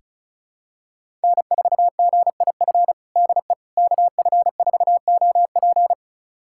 Morse code, heard as a keyed tone, about 36 wpm, 700 Hz. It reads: N4GIF DE KF4OP
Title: USA Callsign DE Callsign - 5 Characters - 35wpm